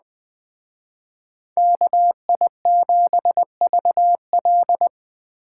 Morse code, heard as a keyed tone, about 20 wpm, 700 Hz. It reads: KI7VL
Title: Speed Racing - USA 2x2 Call Signs 17wpm